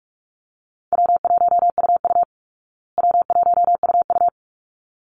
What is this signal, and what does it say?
Morse code, heard as a keyed tone, about 44 wpm, 700 Hz. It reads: W1UU W1UU
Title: Easy Call Signs Repeated 2x - 45wpm